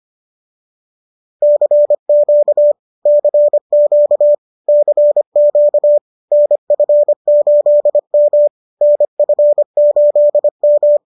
Morse code, heard as a keyed tone, about 25 wpm, 590 Hz. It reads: CQ CQ CQ NF8M NF8M
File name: qso221 25